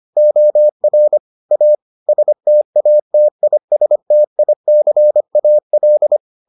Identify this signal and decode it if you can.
Morse code, heard as a keyed tone, about 25 wpm, 600 Hz. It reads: OR A STATISTICAL